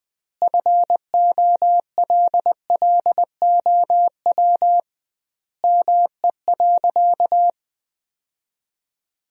Morse code, heard as a keyed tone, about 20 wpm, 700 Hz. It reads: FOLLOW ME.
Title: 2x Word-Spacing - Sentences from Top 300 Words - 20wpm